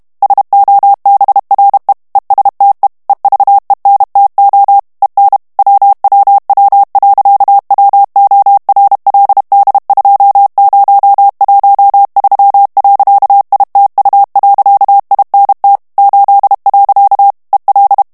Morse code, heard as a keyed tone, about 32 wpm, 800 Hz. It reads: SOBRE ESTE EVENTO EN WWW.WORLD2013.ITU.INT 8. EL
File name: morse mp3 / ESPAGNOL TEXTE EN CLAIR 1 VITESSE 30 MOTS MINUTES